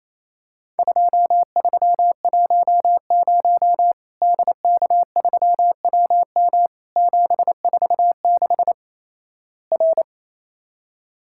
Morse code, heard as a keyed tone, about 28 wpm, 700 Hz. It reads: 2310 DK3WM 746 R
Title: Worked All Europe DX Contest - QTC Exchange - 28wpm